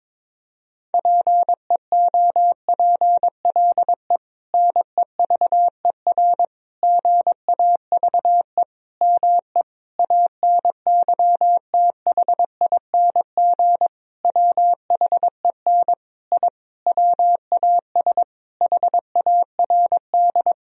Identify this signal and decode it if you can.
Morse code, heard as a keyed tone, about 22 wpm, 700 Hz. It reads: PEOPLE NEVER GAVE ME ANYTHING WHEN I WAS HARD
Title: Sentences from Top 1000 Words - Review Part 1 - 22wpm